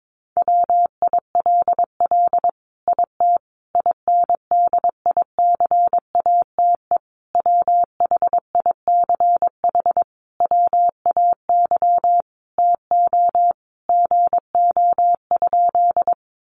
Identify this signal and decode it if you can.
Morse code, heard as a keyed tone, about 22 wpm, 700 Hz. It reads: WILL IT INDICATE WHICH WAY TO GO?
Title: Sentences from Top 1300 Words 22wpm